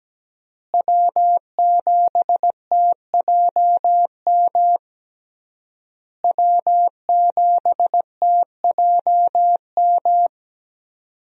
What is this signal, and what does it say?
Morse code, heard as a keyed tone, about 17 wpm, 700 Hz. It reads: W7TJM W7TJM
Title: USA Call Signs - 5 Characters - Repeated 3x - 17wpm